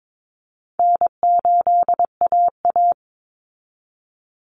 Morse code, heard as a keyed tone, about 22 wpm, 700 Hz. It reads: N8AA
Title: Call Signs Repeated 2x - 22wpm